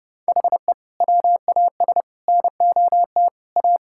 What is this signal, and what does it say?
Morse code, heard as a keyed tone, about 30 wpm, 700 Hz. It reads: HE WAS NOT A
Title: Sentences from Top 1000 Words - Review Part 5 - 30wpm